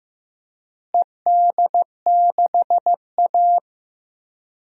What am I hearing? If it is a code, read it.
Morse code, heard as a keyed tone, about 15 wpm, 700 Hz. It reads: ED6A
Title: DXCC-100 Call Signs - 4 Characters - Repeated 3x - 15wpm